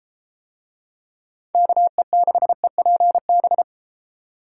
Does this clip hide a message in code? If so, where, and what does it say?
Yes: Morse code from 1.6 to 3.6 s, heard as a keyed tone, about 30 wpm, 700 Hz. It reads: KE6EPB